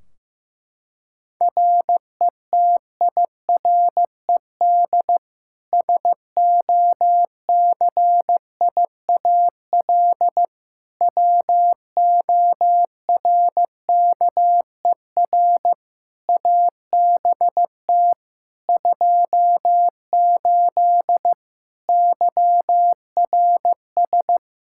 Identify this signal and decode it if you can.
Morse code, heard as a keyed tone, about 15 wpm, 700 Hz. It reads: RETIRED SOCIAL WORKER ABT 28 YRS
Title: K1USN SST QSO Practice 15wpm